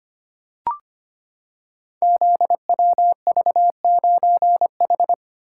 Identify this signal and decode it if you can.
Morse code, heard as a keyed tone, about 25 wpm, 690 Hz. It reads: ZWV9H
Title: Code Groups - 5 Letter:Numbers - 25wpm